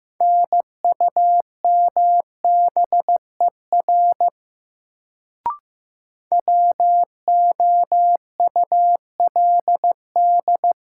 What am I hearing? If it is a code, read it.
Morse code, heard as a keyed tone, about 16 wpm, 700 Hz. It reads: NUMBER WOULD
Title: Sets of 2 Words - New Word Drill - 15wpm